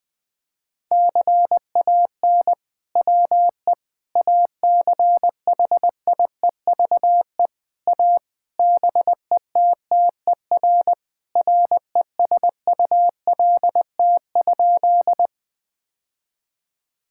Morse code, heard as a keyed tone, about 20 wpm, 700 Hz. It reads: CAN WE ACHIEVE A BETTER RESULT?
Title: Sentences from Top 1000 Words - Review Part 6 - 20wpm